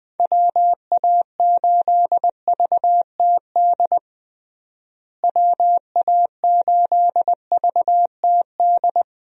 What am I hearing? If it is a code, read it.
Morse code, heard as a keyed tone, about 20 wpm, 700 Hz. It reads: WA8VTD WA8VTD
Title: USA Call Signs - 6 Characters - Repeated 2x - 20wpm